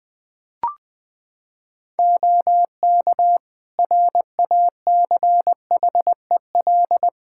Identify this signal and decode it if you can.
Morse code, heard as a keyed tone, about 20 wpm, 700 Hz. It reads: OK RACHEL
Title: Practice QSO - Exchange 2 - Info - 20wpm